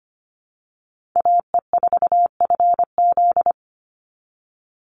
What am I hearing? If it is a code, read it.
Morse code, heard as a keyed tone, about 25 wpm, 700 Hz. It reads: AE4FZ
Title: NA SOTA Chaser Callsigns 25wpm